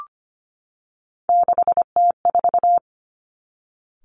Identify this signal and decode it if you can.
Morse code, heard as a keyed tone, about 25 wpm, 700 Hz. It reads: 6T4
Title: Three Challenging Letter-Number Combinations 25wpm